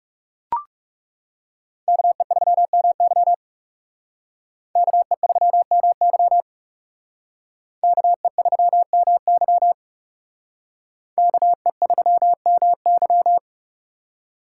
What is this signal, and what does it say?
Morse code, heard as a keyed tone, about 37 wpm, 700 Hz. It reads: KE3MY KE3MY KE3MY KE3MY
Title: Speed Racing - USA 2x2 Call Signs 30wpm